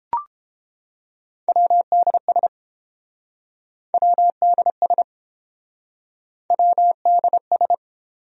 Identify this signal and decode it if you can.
Morse code, heard as a keyed tone, about 31 wpm, 700 Hz. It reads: WDS WDS WDS